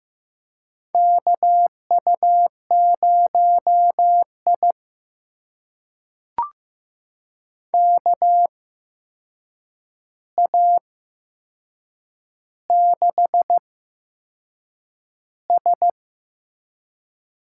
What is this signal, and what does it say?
Morse code, heard as a keyed tone, about 15 wpm, spaced as for 6 wpm, 700 Hz. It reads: KU0I KA6S
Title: USA Call Signs - 4 Characters - 8x Character Spacing - 15wpm